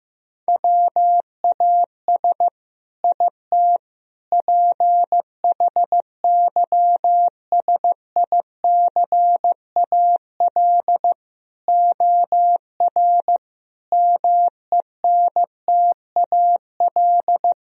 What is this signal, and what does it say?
Morse code, heard as a keyed tone, about 15 wpm, 700 Hz. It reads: WAS IT PHYSICAL OR MENTAL
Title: Sentences from Top 1000 Words - Review Part 4 - 15wpm